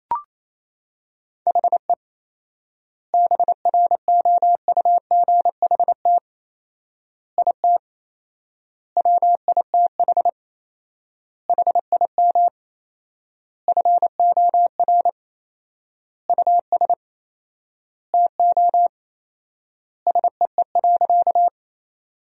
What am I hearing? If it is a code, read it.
Morse code, heard as a keyed tone, about 27 wpm, 700 Hz. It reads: HE BROUGHT IT WITH HIM FOR US TO SEE.
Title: 4x Word-Spacing - Sentences from Top 500 Words - 28wpm